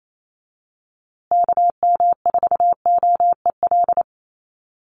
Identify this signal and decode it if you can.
Morse code, heard as a keyed tone, about 28 wpm, 700 Hz. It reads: KM4OEL
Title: USA Call Signs Repeated 2x 28wpm